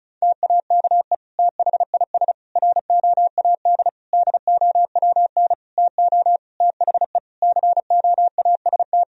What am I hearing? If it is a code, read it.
Morse code, heard as a keyed tone, about 34 wpm, 700 Hz. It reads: TAKE THIS ROAD DOWN TO THE COAST
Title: Sentences from Top 1000 Words - Review Part 4 - 35wpm